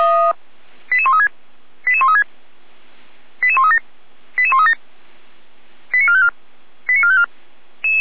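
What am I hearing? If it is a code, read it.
Five-tone selective call (ZVEI), heard as quick runs of five tones, tones of about 80 ms. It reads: 88136 88136 88137 88137 89454 89454
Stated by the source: ZVEI five-tone selective call received via SDR.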